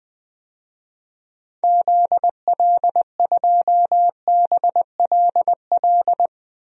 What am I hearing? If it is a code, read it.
Morse code, heard as a keyed tone, about 20 wpm, 700 Hz. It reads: ZL2BLL
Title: DXCC-100 Call Signs - 6 Characters - 20wpm